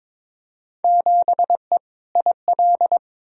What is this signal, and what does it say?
Morse code, heard as a keyed tone, about 22 wpm, 700 Hz. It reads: 7E IL
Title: Field Day Exchanges Encore - 22wpm